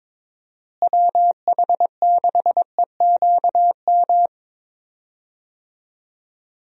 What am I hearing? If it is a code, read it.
Morse code, heard as a keyed tone, about 22 wpm, 700 Hz. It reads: WH6EQM
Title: USA Call Signs - 6 Characters - Repeated 2x - 22wpm